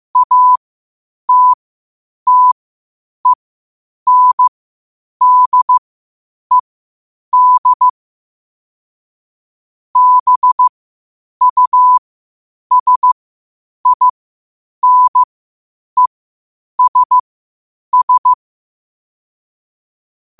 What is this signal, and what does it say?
Morse code, heard as a keyed tone, about 15 wpm, spaced as for 8 wpm, 1000 Hz. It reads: ATTENDED BUSINESS
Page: Quotes for Sat, 16 Aug 2025 in Morse Code at 8 words per minute.